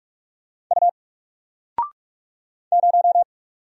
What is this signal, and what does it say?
Morse code, heard as a keyed tone, about 44 wpm, 700 Hz. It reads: U 0
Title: Single Letter-Number Alt 45wpm